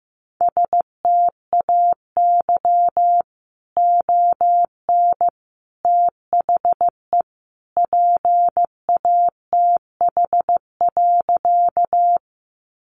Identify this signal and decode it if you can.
Morse code, heard as a keyed tone, about 15 wpm, 700 Hz. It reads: STAY ON THE PATH.
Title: A Sound Of Thunder 15wpm